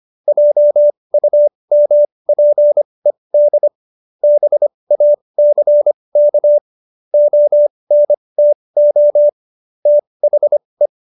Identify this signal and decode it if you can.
Morse code, heard as a keyed tone, about 25 wpm, spaced as for 20 wpm, 590 Hz. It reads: JUMPED BACK ONTO THE